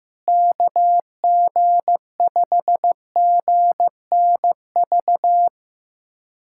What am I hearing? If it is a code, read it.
Morse code, heard as a keyed tone, about 15 wpm, 700 Hz. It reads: KG5GNV
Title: SOTA QSO Practice 15wpm